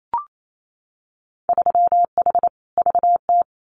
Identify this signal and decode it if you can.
Morse code, heard as a keyed tone, about 28 wpm, 700 Hz. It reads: 3H VT
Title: Winter Field Day Exchanges Encore 28wpm